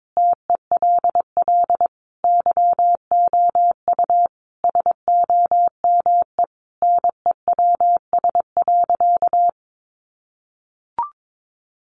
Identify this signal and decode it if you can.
Morse code, heard as a keyed tone, about 22 wpm, 700 Hz. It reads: TELL YOU SOME NEWS.
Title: Sentences from Top 800 Words 22wpm